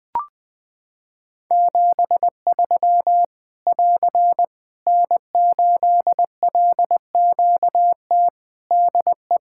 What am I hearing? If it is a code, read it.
Morse code, heard as a keyed tone, about 20 wpm, 700 Hz. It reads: 73 <AR> N8LQT DE
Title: K1USN SST QSO Practice 20wpm